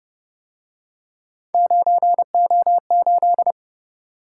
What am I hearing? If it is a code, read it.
Morse code, heard as a keyed tone, about 30 wpm, 700 Hz. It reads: 9O8